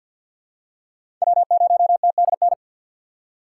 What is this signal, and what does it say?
Morse code, heard as a keyed tone, about 47 wpm, 700 Hz. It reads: W0TDN
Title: USA Call Signs - 5 Characters - 6x Character Spacing - 50wpm